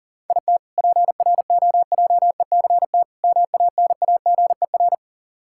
Morse code, heard as a keyed tone, about 37 wpm, 700 Hz. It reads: IT PROJECT MANAGER
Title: K1USN SST QSO Practice 40wpm